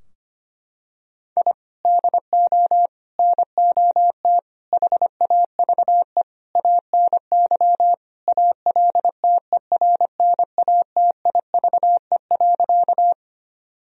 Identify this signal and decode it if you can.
Morse code, heard as a keyed tone, about 25 wpm, 700 Hz. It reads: I DO NOT HAVE ANY ALTERNATIVE.
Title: Sentences from Top 1000 Words - Review Part 4 - 25wpm